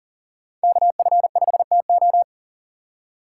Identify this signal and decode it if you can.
Morse code, heard as a keyed tone, about 37 wpm, 700 Hz. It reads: KF5TO
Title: USA Call Signs - 5 Characters - 4x Character Spacing - 40wpm